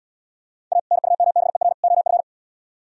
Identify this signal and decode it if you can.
Morse code, heard as a keyed tone, about 70 wpm, 700 Hz. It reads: A NUMBER OF